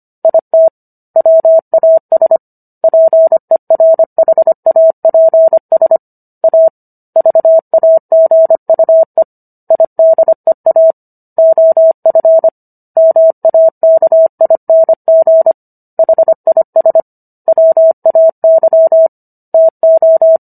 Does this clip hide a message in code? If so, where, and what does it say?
Yes: Morse code from 0.2 to 20.4 s, heard as a keyed tone, about 25 wpm, 650 Hz. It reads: IT WAS PERHAPS A VAGUE IDEA OF MAKING HIS WAY TO